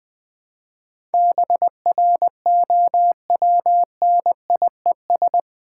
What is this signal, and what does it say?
Morse code, heard as a keyed tone, about 20 wpm, 700 Hz. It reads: BROWNIES
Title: Difficult Words with Hints - 20wpm